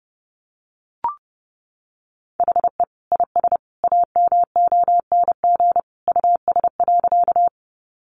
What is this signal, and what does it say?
Morse code, heard as a keyed tone, about 30 wpm, 700 Hz. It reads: HE IS AMONG US.
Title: Sentences from Top 500 Words 30wpm